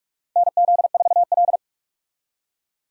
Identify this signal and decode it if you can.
Morse code, heard as a keyed tone, about 46 wpm, 700 Hz. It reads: NZ4L
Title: USA Call Signs - 4 Characters - 45wpm